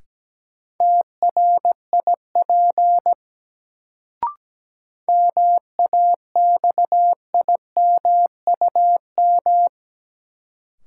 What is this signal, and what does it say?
Morse code, heard as a keyed tone, about 17 wpm, 700 Hz. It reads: TRIP MAXIMUM
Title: Sentences from Top 1300 Words - New Word Drill Alt - 17wpm